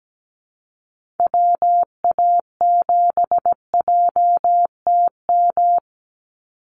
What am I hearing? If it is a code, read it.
Morse code, heard as a keyed tone, about 17 wpm, 700 Hz. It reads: WA7JTM
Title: NA SOTA Chaser Callsigns 17wpm